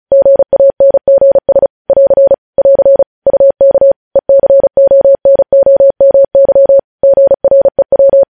Morse code, heard as a keyed tone, about 36 wpm, 550 Hz. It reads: GANGS <AR> <AR> UK ECONOMY GREW
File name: podcast / News headlines-1x-35x35